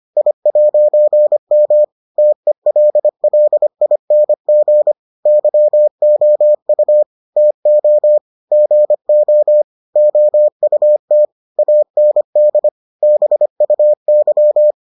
Morse code, heard as a keyed tone, about 25 wpm, 600 Hz. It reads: I'M TELLING YOU TO GO OUT AND BUY